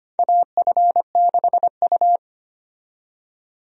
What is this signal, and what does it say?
Morse code, heard as a keyed tone, about 25 wpm, 700 Hz. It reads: AF6U
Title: USA Call Signs - 4 Characters - 25wpm